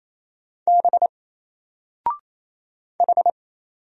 Morse code, heard as a keyed tone, about 28 wpm, 700 Hz. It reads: B H
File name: Single-Letter-Number-28wpm